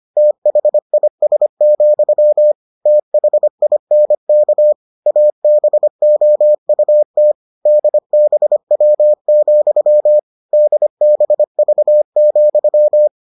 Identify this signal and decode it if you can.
Morse code, heard as a keyed tone, about 25 wpm, 600 Hz. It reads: THIS, THINK ABOUT DBW, DBV,